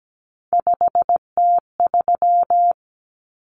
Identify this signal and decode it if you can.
Morse code, heard as a keyed tone, about 17 wpm, 700 Hz. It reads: 5T3
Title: Common Cut Numbers - 3-Digit 17wpm